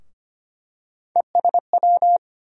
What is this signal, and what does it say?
Morse code, heard as a keyed tone, about 25 wpm, 700 Hz. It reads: ESW
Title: USA Call Sign Postfixes - 3 Letters 25wpm